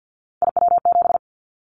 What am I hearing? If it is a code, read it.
Morse code, heard as a keyed tone, about 50 wpm, 700 Hz. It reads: IW7
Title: Callsign Prefixes for Top 25 DXCC - 50wpm